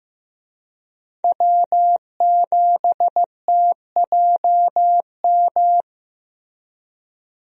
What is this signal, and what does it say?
Morse code, heard as a keyed tone, about 16 wpm, 700 Hz. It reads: W7TJM
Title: USA Call Signs - 5 Characters - Repeated 3x - 15wpm